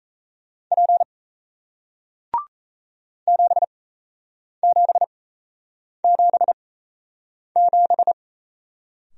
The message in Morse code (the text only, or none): P 7777